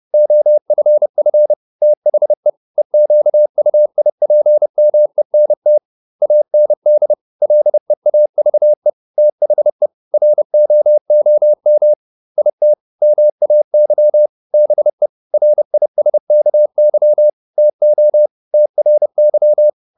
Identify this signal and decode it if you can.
Morse code, heard as a keyed tone, about 30 wpm, 600 Hz. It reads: OFF THE EQUIPMENT AND LEAVE THE ROOM IT MAY BE RISKY TO TRY